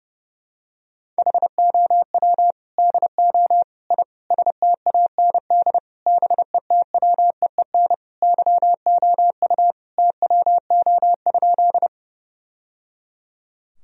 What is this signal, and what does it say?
Morse code, heard as a keyed tone, about 30 wpm, 700 Hz. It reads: HOW DO I STAND BETWEEN YOU TWO?
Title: Sentences from Top 1000 Words - Review Part 1 - 30wpm